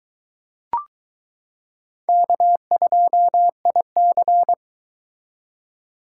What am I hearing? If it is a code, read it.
Morse code, heard as a keyed tone, about 22 wpm, 700 Hz. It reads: K2IC